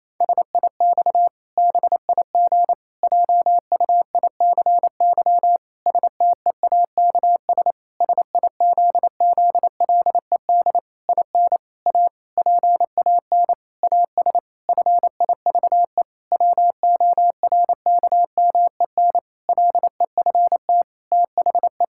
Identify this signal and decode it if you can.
Morse code, heard as a keyed tone, about 27 wpm, 700 Hz. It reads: SIX BIG JUICY STEAKS SIZZLED IN A PAN AS FIVE WORKMEN LEFT THE
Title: Pangram Challenge v2 - 28wpm